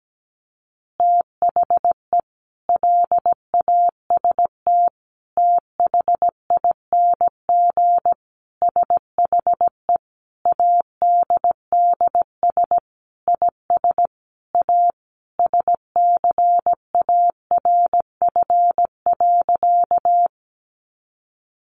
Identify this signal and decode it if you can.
Morse code, heard as a keyed tone, about 17 wpm, 700 Hz. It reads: THE LAST THING SHE ADDS IS A SCARF.
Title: Easy Phrases 17wpm